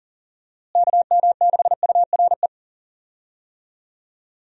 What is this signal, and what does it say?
Morse code, heard as a keyed tone, about 39 wpm, 700 Hz. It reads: KM6URE
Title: USA Call Signs - 6 Characters - Repeated 3x - 40wpm